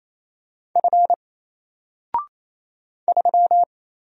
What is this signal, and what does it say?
Morse code, heard as a keyed tone, about 27 wpm, 700 Hz. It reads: F 3
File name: Single-Letter-Number-28wpm